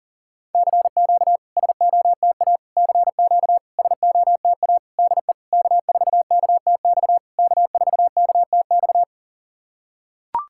Morse code, heard as a keyed tone, about 40 wpm, 690 Hz. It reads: CQ SOTA CQ SOTA DE K4KTX K4KTX
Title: SOTA QSO Practice 40wpm